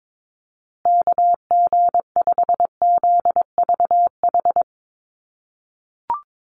Morse code, heard as a keyed tone, about 22 wpm, 700 Hz. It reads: KG5ZVH
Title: USA Call Signs Repeated 2x 22wpm